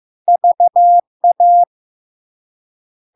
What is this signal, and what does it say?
Morse code, heard as a keyed tone, about 15 wpm, 700 Hz. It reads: VA
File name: LARK-FD-2022-Log-in-CW-15wpm-700-Hz